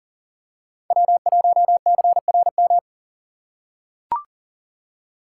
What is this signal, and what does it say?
Morse code, heard as a keyed tone, about 40 wpm, 700 Hz. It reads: W1CRM
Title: USA Call Signs - 5 Characters - 3x Character Spacing - 40wpm